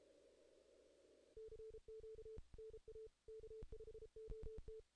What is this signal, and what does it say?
Morse code, heard as a keyed tone, about 24 wpm, 440 Hz. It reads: CQ NA K5OT
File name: 2016 JAN CW NAQP / 2016 Jan naqp k5ot - 0-11